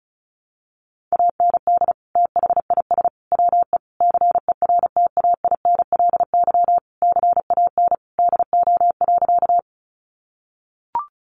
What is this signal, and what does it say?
Morse code, heard as a keyed tone, about 34 wpm, 700 Hz. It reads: AND THIS WE CERTAINLY CAN DO.
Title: Sentences from Top 900 Words 35wpm